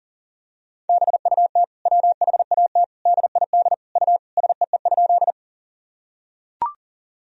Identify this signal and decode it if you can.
Morse code, heard as a keyed tone, about 37 wpm, 700 Hz. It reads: BUT WHAT DID U SEE?
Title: Sentences from Top 100 Words and Abbreviations 40wpm